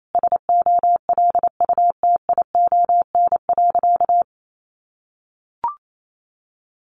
Morse code, heard as a keyed tone, about 28 wpm, 700 Hz. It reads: SOLUTION.
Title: Sentences from Top 1300 Words 28wpm